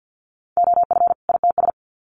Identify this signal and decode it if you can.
Morse code, heard as a keyed tone, about 50 wpm, 700 Hz. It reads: OF ITS
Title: Sets of 2 Words 50wpm